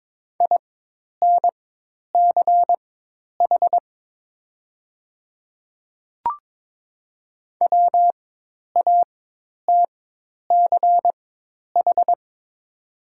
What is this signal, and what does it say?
Morse code, heard as a keyed tone, about 22 wpm, spaced as for 10 wpm, 700 Hz. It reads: INCH WATCH
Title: 4x Character Spacing - Top 500 Words - 22wpm